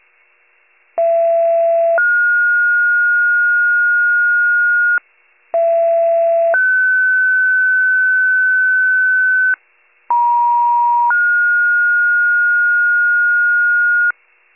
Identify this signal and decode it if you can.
Two-tone page, heard as a one-second tone, then a three-second tone, about 670 then 1430 Hz, about 670 then 1500 Hz, about 950 then 1430 Hz. It is heard from 1.0 to 14.1 s.
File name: pager / 39 FIRE / wcalert